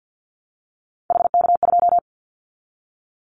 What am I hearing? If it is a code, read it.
Morse code, heard as a keyed tone, about 50 wpm, 700 Hz. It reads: HK2
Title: Callsign Prefixes for Top 100 DXCC - 50wpm